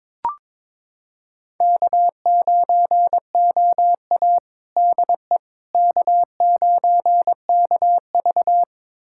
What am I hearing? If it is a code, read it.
Morse code, heard as a keyed tone, about 22 wpm, 700 Hz. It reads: K9OA DE K9KV
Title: USA Callsign DE Callsign - 4 Characters - 22wpm